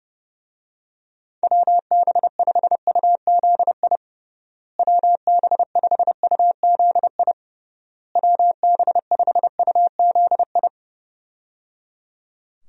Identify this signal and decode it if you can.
Morse code, heard as a keyed tone, about 30 wpm, 700 Hz. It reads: WB5UZI WB5UZI WB5UZI
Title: USA Call Signs - 6 Characters - Repeated 3x - 30wpm